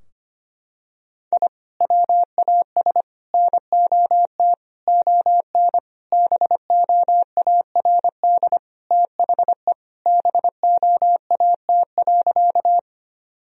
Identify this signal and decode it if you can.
Morse code, heard as a keyed tone, about 25 wpm, 700 Hz. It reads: I WAS NOT ON BOARD THE BOAT.
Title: Sentences from Top 1000 Words - Review Part 4 - 25wpm